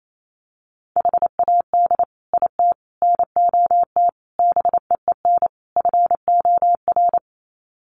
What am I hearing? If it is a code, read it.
Morse code, heard as a keyed tone, about 28 wpm, 700 Hz. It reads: HAD IT NOT BEEN FOR
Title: Sets of 5 Words Encore2 28wpm